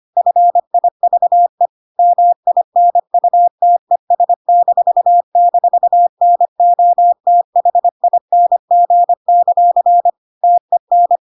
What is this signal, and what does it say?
Morse code, heard as a keyed tone, about 25 wpm, 700 Hz. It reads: FIVE MINUTES--NOTHING; TEN
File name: Pinocchio / Pinocchio ch1 25 25 ik1hge